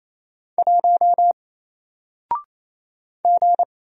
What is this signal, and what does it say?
Morse code, heard as a keyed tone, about 27 wpm, 700 Hz. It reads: 1 G
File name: Single-Letter-Number-28wpm